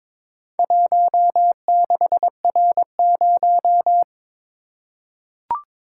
Morse code, heard as a keyed tone, about 22 wpm, 700 Hz. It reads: 16R0
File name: practice / 22wpm / Four Letter-Number Combinations - 22wpm